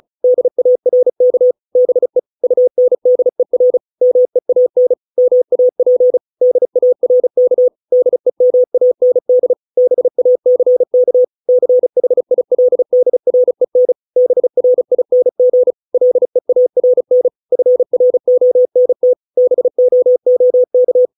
Morse code, heard as a keyed tone, about 34 wpm, 500 Hz. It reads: DARK BE UNDER MEAN MAP DARK DEMAND BACK CHILDREN BRING LEARN FRONT BOOK